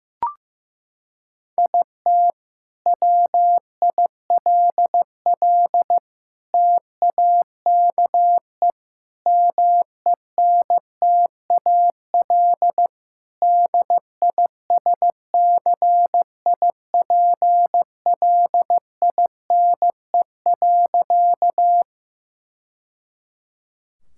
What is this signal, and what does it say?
Morse code, heard as a keyed tone, about 15 wpm, 700 Hz. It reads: IT WILL TAKE MENTAL DISCIPLINE.
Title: Sentences from Top 1000 Words - Review Part 6 - 15wpm